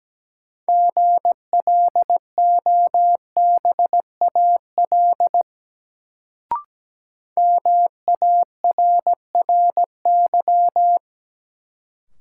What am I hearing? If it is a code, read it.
Morse code, heard as a keyed tone, about 17 wpm, 700 Hz. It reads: GLOBAL MARRY
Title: Sentences from Top 1200 Words - New Word Drill - 17wpm